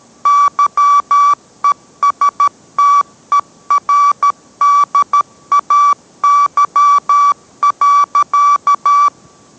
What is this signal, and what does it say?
Morse code, heard as a keyed tone, about 13 wpm, 1150 Hz. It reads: YESTERDAY.